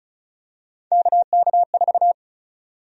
Morse code, heard as a keyed tone, about 36 wpm, 700 Hz. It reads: KK4
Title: USA Call Sign Prefixes - 35wpm